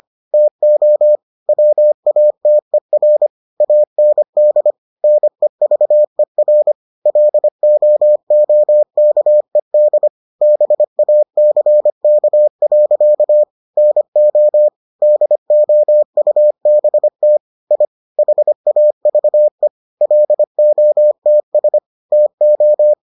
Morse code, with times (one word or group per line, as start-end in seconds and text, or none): 0.3-1.2 TO
1.5-3.3 WATER
3.6-4.7 AND
5.0-6.7 NEVER
7.1-10.1 LOOKED
10.4-13.4 BACK.
13.8-14.7 NO
15.0-17.4 DOUBT
17.7-17.9 I
18.2-19.7 HAVE
20.0-21.8 LOTS
22.1-22.9 TO